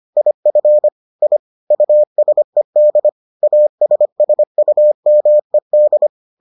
Morse code, heard as a keyed tone, about 25 wpm, 600 Hz. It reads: IF I USED ASSUMED